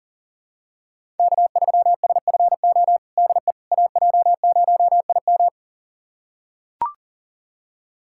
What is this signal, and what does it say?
Morse code, heard as a keyed tone, about 40 wpm, 700 Hz. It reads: K3SFO DE AJ0IM
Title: USA Callsign DE Callsign - 5 Characters - 40wpm